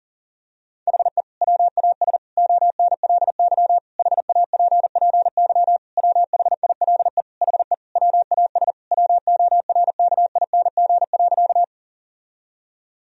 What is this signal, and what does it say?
Morse code, heard as a keyed tone, about 40 wpm, 700 Hz. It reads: HE WAS ONLY HAPPY WHILE HE WAS WORKING.
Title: Sentences from Top 1000 Words - Review Part 4 - 40wpm